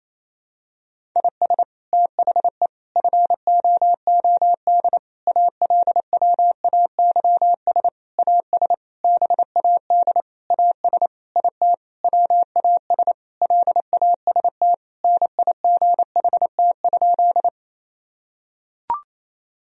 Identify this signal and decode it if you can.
Morse code, heard as a keyed tone, about 28 wpm, 700 Hz. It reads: IS THE FOOD ALWAYS AS BAD AS IT WAS LAST NIGHT?
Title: Sentences from Top 1000 Words - Review Part 1 - 28wpm